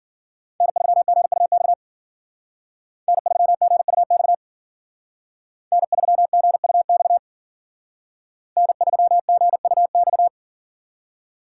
Morse code, heard as a keyed tone, about 52 wpm, 700 Hz. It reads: N3GUX N3GUX N3GUX N3GUX